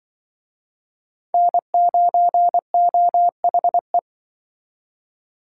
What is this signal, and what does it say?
Morse code, heard as a keyed tone, about 25 wpm, 700 Hz. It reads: N9OHE
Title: Speed Racing - USA 1x3 Call Signs 20wpm